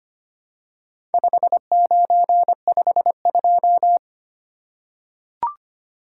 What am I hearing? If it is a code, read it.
Morse code, heard as a keyed tone, about 25 wpm, 700 Hz. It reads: 5952
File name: practice / 25wpm / Four Letter-Number Combinations - 25wpm